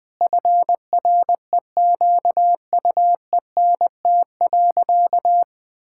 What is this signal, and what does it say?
Morse code, heard as a keyed tone, about 20 wpm, 700 Hz. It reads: FREQUENT.
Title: Sentences with 6-Word Phrases - 20wpm